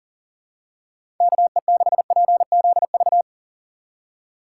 Morse code, heard as a keyed tone, about 40 wpm, 700 Hz. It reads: KE6PZV
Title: USA Call Signs - 6 Characters - 3x Character Spacing - 40wpm